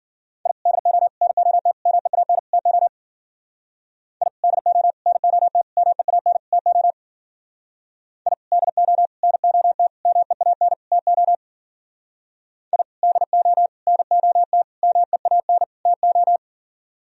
Morse code, heard as a keyed tone, about 50 wpm, 700 Hz. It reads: I DO NOT MEAN TO I DO NOT MEAN TO I DO NOT MEAN TO I DO NOT MEAN TO
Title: Speed-Racing - Sets of 5 Words 40wpm